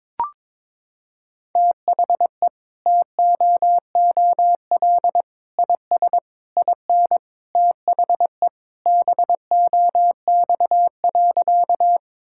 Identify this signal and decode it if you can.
Morse code, heard as a keyed tone, about 22 wpm, 700 Hz. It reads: THE TOOL IS IN THE BOX.
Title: Sentences from Top 1000 Words - Review Part 6 - 22wpm